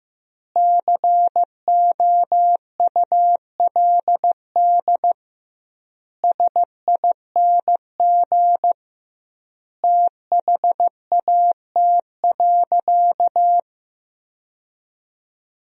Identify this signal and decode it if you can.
Morse code, heard as a keyed tone, about 16 wpm, 700 Hz. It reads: COULD SING THAT.
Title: 2x Word-Spacing - Sentences from Top 500 Words - 15wpm